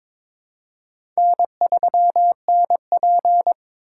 Morse code, heard as a keyed tone, about 22 wpm, 700 Hz. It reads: N3NP
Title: USA Call Signs - 4 Characters - 4x Character Spacing - 22wpm